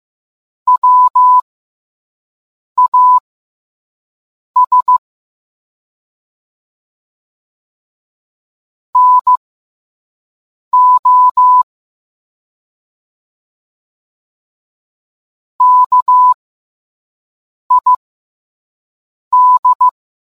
Morse code, heard as a keyed tone, about 15 wpm, spaced as for 5 wpm, 1000 Hz. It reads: WAS NO KID